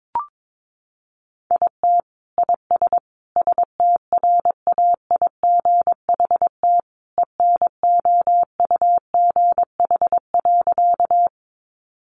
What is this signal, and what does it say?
Morse code, heard as a keyed tone, about 22 wpm, 700 Hz. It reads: IT IS STRAIGHT ENOUGH.
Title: Sentences from Top 1000 Words 22wpm